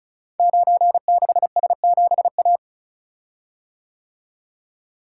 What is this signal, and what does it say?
Morse code, heard as a keyed tone, about 36 wpm, 690 Hz. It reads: 96S7A
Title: Code Groups - 5 Letter:Numbers - 35wpm